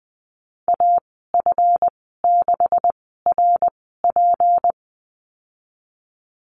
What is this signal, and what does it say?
Morse code, heard as a keyed tone, about 20 wpm, spaced as for 15 wpm, 700 Hz. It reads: AF6RP
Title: 167 Lesson 27g - Number 7 - Call Signs 20wpm